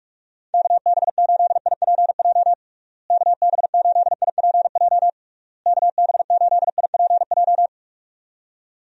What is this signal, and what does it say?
Morse code, heard as a keyed tone, about 46 wpm, 700 Hz. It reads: KB8IPJ KB8IPJ KB8IPJ
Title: USA Call Signs - 6 Characters - Repeated 3x - 45wpm